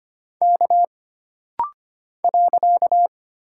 Morse code, heard as a keyed tone, about 25 wpm, 700 Hz. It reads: K .
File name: Single Character (Letter:Number:Punctuation:Prosign) - Rapid-Fire - 25wpm